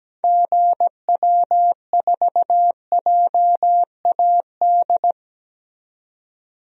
Morse code, heard as a keyed tone, about 17 wpm, 700 Hz. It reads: GW4JAD
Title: DXCC-100 Call Signs - 6 Characters - 17wpm